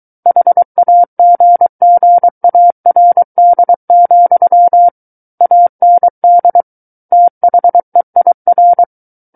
Morse code, heard as a keyed tone, about 23 wpm, 700 Hz. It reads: HAGGARD, AND THEIR
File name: Book One / 14-Chapter 14 - 23 WPM